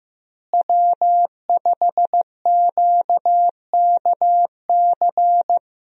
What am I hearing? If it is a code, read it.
Morse code, heard as a keyed tone, about 15 wpm, 700 Hz. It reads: W5QKC
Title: USA Call Signs - 5 Characters - Repeated 3x - 15wpm